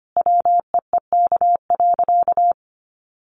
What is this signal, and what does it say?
Morse code, heard as a keyed tone, about 25 wpm, 700 Hz. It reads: WEEK.
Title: Sentences from Top 1400 Words 25wpm